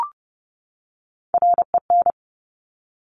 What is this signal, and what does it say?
Morse code, heard as a keyed tone, about 30 wpm, 700 Hz. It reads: REN
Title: Top Three Letter Combinations in Words 30wpm